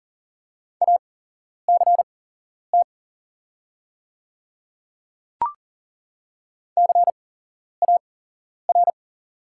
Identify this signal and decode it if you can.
Morse code, heard as a keyed tone, about 40 wpm, spaced as for 11 wpm, 700 Hz. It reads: ACT CAR
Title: 8x Character Spacing - Top 300 Words - 40wpm